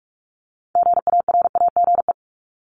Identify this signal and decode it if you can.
Morse code, heard as a keyed tone, about 44 wpm, 700 Hz. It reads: GARAGE
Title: Top 1501-1750 Words 45wpm